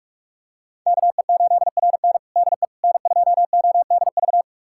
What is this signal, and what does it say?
Morse code, heard as a keyed tone, about 46 wpm, 700 Hz. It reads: KE8RN DE N2ODV
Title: USA Callsign DE Callsign - 5 Characters - 45wpm